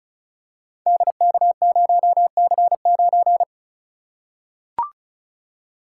Morse code, heard as a keyed tone, about 36 wpm, 700 Hz. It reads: DK0C9
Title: Code Groups - 5 Letter:Numbers - 35wpm